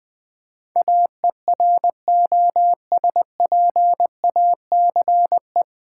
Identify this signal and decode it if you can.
Morse code, heard as a keyed tone, about 20 wpm, 700 Hz. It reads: AEROSPACE
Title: Practice QSO - Exchange 3 - More Info - 20wpm